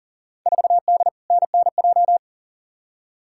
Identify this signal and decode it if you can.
Morse code, heard as a keyed tone, about 40 wpm, 700 Hz. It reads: 4D NNJ
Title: Field Day Exchanges Encore - 40wpm